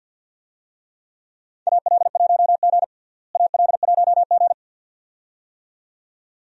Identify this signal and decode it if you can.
Morse code, heard as a keyed tone, about 51 wpm, 700 Hz. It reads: AL1G AL1G
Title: USA Call Signs - 4 Characters - Repeated 2x - 50wpm